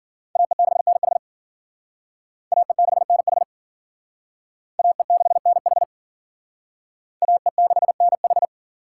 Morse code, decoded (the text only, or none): AE6NH AE6NH AE6NH AE6NH